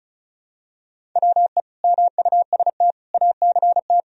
Morse code, heard as a keyed tone, about 36 wpm, 700 Hz. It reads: WE MUST ACT
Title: Sentences from Top 1000 Words - Review Part 1 - 35wpm